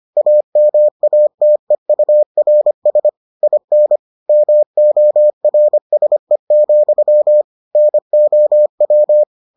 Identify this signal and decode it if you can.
Morse code, heard as a keyed tone, about 25 wpm, 600 Hz. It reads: AMATEURS IN MORSE, NOW